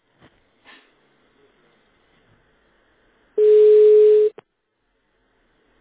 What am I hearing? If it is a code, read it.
Ringback tone (Europe, 425 Hz).